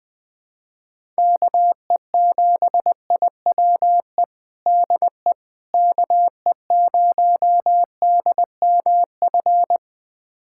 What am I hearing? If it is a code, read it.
Morse code, heard as a keyed tone, about 20 wpm, 700 Hz. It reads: KE7IWE DE KE0DMF
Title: USA Callsign DE Callsign 20wpm